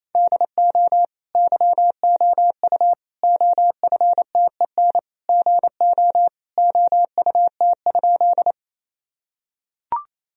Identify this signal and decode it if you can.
Morse code, heard as a keyed tone, about 28 wpm, 700 Hz. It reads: DO YOU OFTEN GO OUT?
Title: Sentences from Top 1000 Words - Review Part 1 - 28wpm